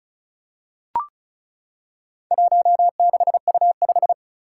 Morse code, heard as a keyed tone, about 36 wpm, 700 Hz. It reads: 16U5
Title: Four Challenging Letter-Number Combinations 35wpm